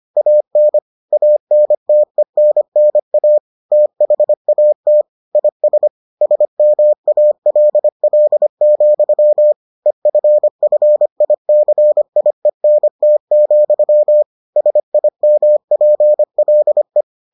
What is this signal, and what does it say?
Morse code, heard as a keyed tone, about 25 wpm, 600 Hz. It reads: AN ANTENNA THAT IS SMALL, EFFICIENT, SIMPLE